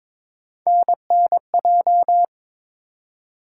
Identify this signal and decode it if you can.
Morse code, heard as a keyed tone, about 21 wpm, 700 Hz. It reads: NNJ
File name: ARRL Sections v2 - 22wpm